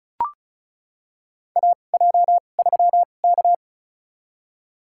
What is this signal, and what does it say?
Morse code, heard as a keyed tone, about 36 wpm, spaced as for 25 wpm, 700 Hz. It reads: AJ3K
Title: USA Call Signs - 2x Character Spacing - 35wpm